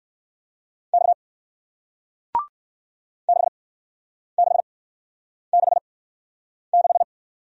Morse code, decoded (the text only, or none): X 6666